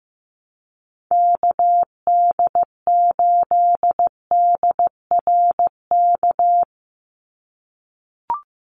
Morse code, heard as a keyed tone, about 15 wpm, 700 Hz. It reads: KD8DRK
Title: USA Call Signs 15wpm